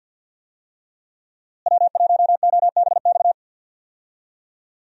Morse code, heard as a keyed tone, about 46 wpm, 700 Hz. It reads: W1OBX
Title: USA Call Signs - 5 Characters - Repeated 3x - 50wpm